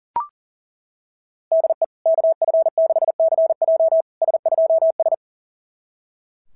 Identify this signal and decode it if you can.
Morse code, heard as a keyed tone, about 40 wpm, 660 Hz. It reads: DE KF6CJ S2S
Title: SOTA QSO Practice 40wpm